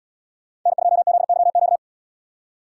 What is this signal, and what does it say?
Morse code, heard as a keyed tone, about 71 wpm, 700 Hz. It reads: N2ZYQ